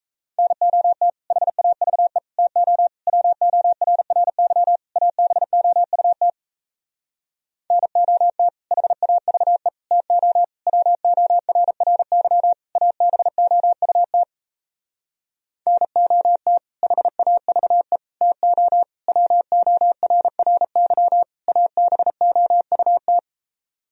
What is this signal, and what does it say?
Morse code, heard as a keyed tone, about 38 wpm, 700 Hz. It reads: NOT HAVE TO WORRY ABOUT NOT HAVE TO WORRY ABOUT NOT HAVE TO WORRY ABOUT